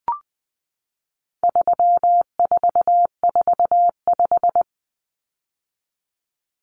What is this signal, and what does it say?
Morse code, heard as a keyed tone, about 20 wpm, 700 Hz. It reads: 3445
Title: Numbers - 4-Digit 20wpm